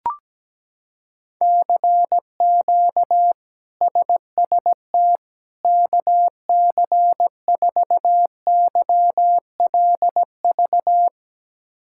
Morse code, heard as a keyed tone, about 17 wpm, 700 Hz. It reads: CQ SST KC4YLV
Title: K1USN SST QSO Practice - 17wpm